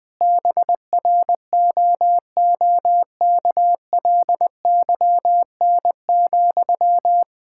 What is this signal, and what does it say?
Morse code, heard as a keyed tone, about 20 wpm, 700 Hz. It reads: BROOKLYN,
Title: Practice QSO - Exchange 1 - Beginning - 20wpm